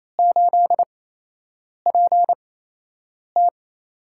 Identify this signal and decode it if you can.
Morse code, heard as a keyed tone, about 27 wpm, spaced as for 8 wpm, 700 Hz. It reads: 8PT
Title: USA Call Signs - 4 Characters - 8x Character Spacing - 28wpm